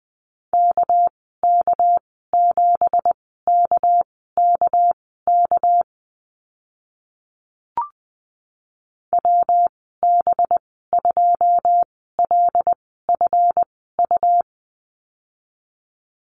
Morse code, heard as a keyed tone, about 20 wpm, spaced as for 14 wpm, 700 Hz. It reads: KK7KKK WB2LFU
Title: 213 Lesson 34g - Number 8 - Call Signs 20wpm